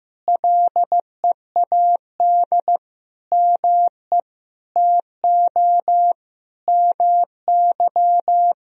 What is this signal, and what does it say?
Morse code, heard as a keyed tone, about 15 wpm, 700 Hz. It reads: LEAD ME TO MY
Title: Sentences from Top 1000 Words - Review Part 2 - 15wpm